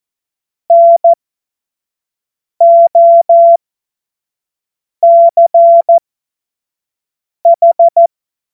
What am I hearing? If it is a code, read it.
Morse code, heard as a keyed tone, about 14 wpm, spaced as for 5 wpm, 680 Hz. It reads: NOCH
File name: Morsekurs / Witze / Witz 02. Februar 2021, 25 F70